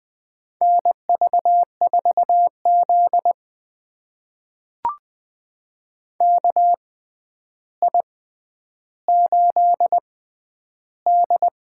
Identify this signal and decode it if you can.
Morse code, heard as a keyed tone, about 20 wpm, spaced as for 10 wpm, 700 Hz. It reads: NV4Z KI8D
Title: USA Call Signs - 6x Character Spacing - 20wpm